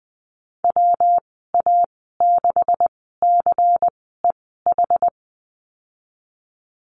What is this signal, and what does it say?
Morse code, heard as a keyed tone, about 20 wpm, spaced as for 15 wpm, 700 Hz. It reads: WA6CEH
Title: 154 Lesson 25g - Letter p - Call Signs 20wpm